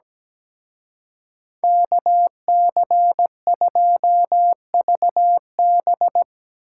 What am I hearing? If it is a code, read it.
Morse code, heard as a keyed tone, about 19 wpm, 700 Hz. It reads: KC2VB
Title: Speed Racing - USA 2x2 Call Signs 17wpm